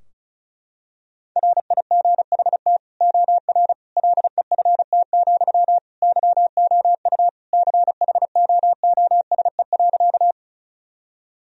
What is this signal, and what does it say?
Morse code, heard as a keyed tone, about 36 wpm, 700 Hz. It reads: RIGHT OR LEFT, YOU CHOOSE.
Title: Sentences from Top 1000 Words - Review Part 4 - 35wpm